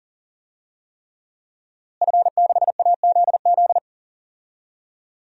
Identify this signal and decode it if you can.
Morse code, heard as a keyed tone, about 37 wpm, 700 Hz. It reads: F6AZZ
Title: DXCC-100 Call Signs - 5 Characters - Repeated 3x - 40wpm